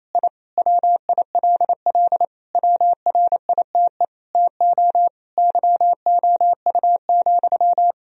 Morse code, heard as a keyed tone, about 28 wpm, 700 Hz. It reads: I WILL WRITE TO YOU,
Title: Sentences from Top 1000 Words - Review Part 1 - 28wpm